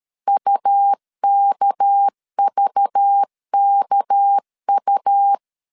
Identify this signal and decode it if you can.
Morse code, heard as a keyed tone, about 13 wpm, 800 Hz. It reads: UKVKU
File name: LEZIONI / 6 PKX0 / 6 PK 12wpm 2006